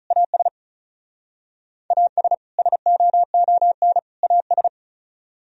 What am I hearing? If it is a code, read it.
Morse code, heard as a keyed tone, about 35 wpm, 700 Hz. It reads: AS AS SOON AS
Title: Speed-Racing - Sets of 3 Words 35wpm